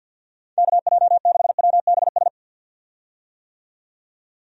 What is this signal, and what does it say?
Morse code, heard as a keyed tone, about 51 wpm, 700 Hz. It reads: KJ6WBS
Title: USA Call Signs - 6 Characters - 50wpm